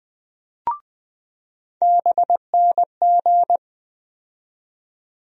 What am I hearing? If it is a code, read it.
Morse code, heard as a keyed tone, about 20 wpm, 700 Hz. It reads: BNG
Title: USA Call Sign Postfixes - 3 Letters 20wpm